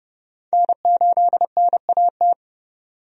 Morse code, heard as a keyed tone, about 29 wpm, 700 Hz. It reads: N8NAT
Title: USA Call Signs - 5 Characters - 30wpm